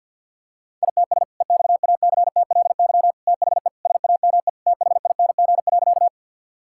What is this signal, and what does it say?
Morse code, heard as a keyed tone, about 50 wpm, 700 Hz. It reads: ITS EXACTLY THE SAME THING.
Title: Sentences from Top 1000 Words - Review Part 6 - 50wpm